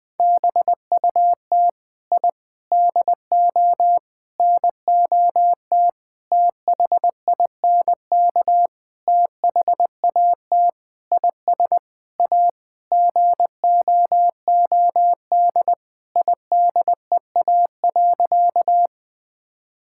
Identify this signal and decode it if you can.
Morse code, heard as a keyed tone, about 20 wpm, 700 Hz. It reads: BUT I DO NOT THINK THAT IS A GOOD IDEA.
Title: Sentences with 5-Word Phrases - 20wpm